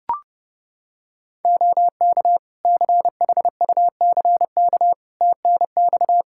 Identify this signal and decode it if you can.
Morse code, heard as a keyed tone, about 30 wpm, 700 Hz. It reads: OK CHUCK TNX
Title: Practice QSO - Exchange 4 - Ending - 30wpm